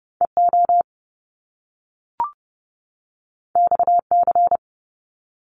Morse code, heard as a keyed tone, about 30 wpm, 700 Hz. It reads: EO XC
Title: Top Two Letter Combinations in Words 30wpm